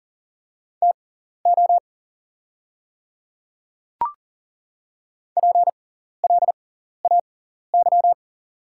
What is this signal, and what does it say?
Morse code, heard as a keyed tone, about 40 wpm, spaced as for 14 wpm, 700 Hz. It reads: TO PLAY
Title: 6x Character Spacing - Top 300 Words - 40wpm